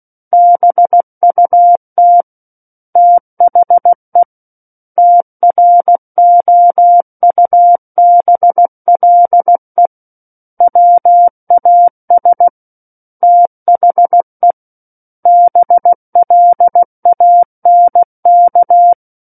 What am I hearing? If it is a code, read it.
Morse code, heard as a keyed tone, about 16 wpm, 700 Hz. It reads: BUT THE TROUBLE WAS THE BLANK
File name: Book One / 07-Chapter 7 - 16 WPM